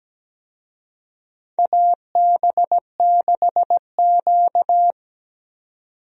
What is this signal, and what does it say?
Morse code, heard as a keyed tone, about 17 wpm, 700 Hz. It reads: AB6Q
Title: USA Call Signs - 4 Characters - 17wpm